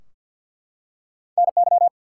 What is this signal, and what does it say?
Morse code, heard as a keyed tone, about 48 wpm, 700 Hz. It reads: NY
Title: US States and Canadian Provinces 50wpm